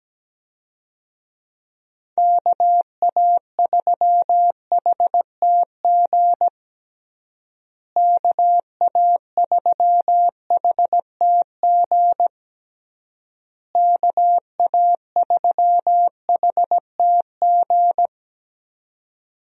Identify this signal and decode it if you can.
Morse code, heard as a keyed tone, about 17 wpm, 700 Hz. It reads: KA3HTG KA3HTG KA3HTG
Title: USA Call Signs - 6 Characters - Repeated 3x - 17wpm